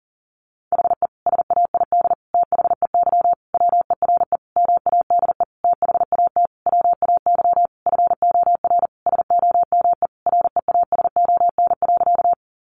Morse code, heard as a keyed tone, about 40 wpm, 700 Hz. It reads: HE SAID THEY WERE MADE THAT WAY FOR SOME REASON.
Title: Sentences from Top 700 Words 40wpm